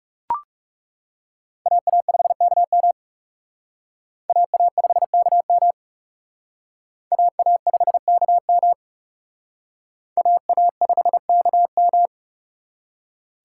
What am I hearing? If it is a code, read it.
Morse code, heard as a keyed tone, about 37 wpm, 700 Hz. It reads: AA5KM AA5KM AA5KM AA5KM
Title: Speed Racing - USA 2x2 Call Signs 30wpm